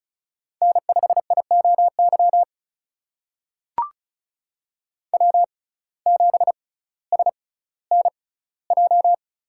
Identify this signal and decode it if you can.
Morse code, heard as a keyed tone, about 36 wpm, spaced as for 18 wpm, 700 Hz. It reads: N5IOY W7SNJ
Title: USA Call Signs - 5 Characters - 6x Character Spacing - 35wpm